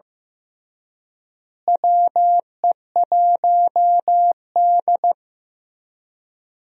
Morse code, heard as a keyed tone, about 15 wpm, 700 Hz. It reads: WE1D
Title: USA Call Signs - 4 Characters - Repeated 3x - 15wpm